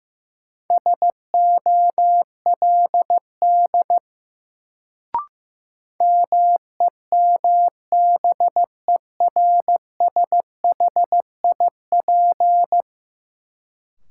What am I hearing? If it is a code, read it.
Morse code, heard as a keyed tone, about 16 wpm, 700 Hz. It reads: SOLD MEMBERSHIP
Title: Sentences from Top 1400 Words - New Word Drill - 15wpm